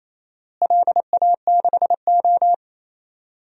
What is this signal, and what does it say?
Morse code, heard as a keyed tone, about 28 wpm, 700 Hz. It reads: LA6O
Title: DXCC-100 Call Signs - 4 Characters - Repeated 2x - 28wpm